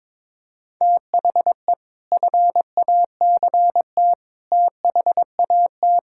Morse code, heard as a keyed tone, about 22 wpm, 700 Hz. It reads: THE FACT THAT
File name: practice / 22wpm / Sentences with 3-Word Phrases - 22wpm